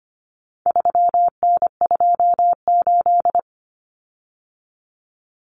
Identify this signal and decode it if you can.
Morse code, heard as a keyed tone, about 25 wpm, 700 Hz. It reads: 3N28
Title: Common Cut Numbers - 4-Digit 25wpm